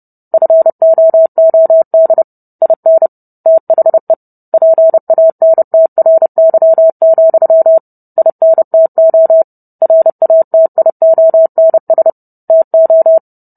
Morse code, heard as a keyed tone, about 30 wpm, 650 Hz. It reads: FOOD IN THE PANTRY, INTO RATIONS TO